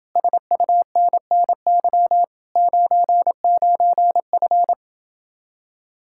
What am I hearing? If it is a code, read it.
Morse code, heard as a keyed tone, about 29 wpm, 700 Hz. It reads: SUNNY 99F